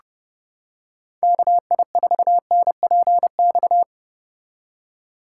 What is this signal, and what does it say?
Morse code, heard as a keyed tone, about 29 wpm, 700 Hz. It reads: KI4NPX